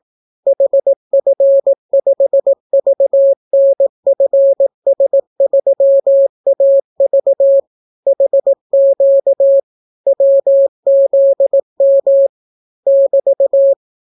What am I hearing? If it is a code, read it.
Morse code, heard as a keyed tone, about 18 wpm, 550 Hz. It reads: HF5VNFS3AV HQ WZM <BT>